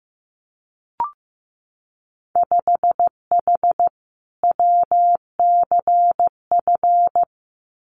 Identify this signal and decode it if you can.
Morse code, heard as a keyed tone, about 15 wpm, 700 Hz. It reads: 5H WCF
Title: Winter Field Day Exchanges Encore 15wpm